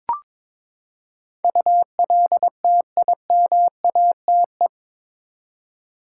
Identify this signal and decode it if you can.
Morse code, heard as a keyed tone, about 22 wpm, 700 Hz. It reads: ULTIMATE
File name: practice / 22wpm / Eight Letter Words - 22wpm